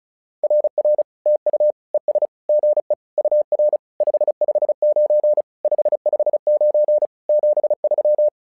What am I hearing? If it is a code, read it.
Morse code, heard as a keyed tone, about 36 wpm, 600 Hz. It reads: RR TU ES GE UR 559 559 73
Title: SOTA QSO Practice 35wpm